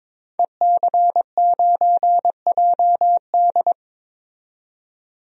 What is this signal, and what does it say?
Morse code, heard as a keyed tone, about 21 wpm, 700 Hz. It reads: EC9JD
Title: DXCC-100 Call Signs - 5 Characters - 22wpm